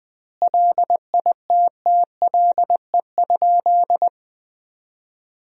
Morse code, heard as a keyed tone, about 20 wpm, 700 Hz. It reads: LITTLE?
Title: 2x Word-Spacing - Sentences from Top 500 Words - 20wpm